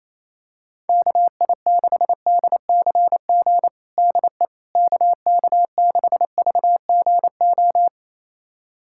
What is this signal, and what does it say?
Morse code, heard as a keyed tone, about 27 wpm, 700 Hz. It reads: KI6DCG DE KK6VGO
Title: USA Callsign DE Callsign - 6 Characters - 28wpm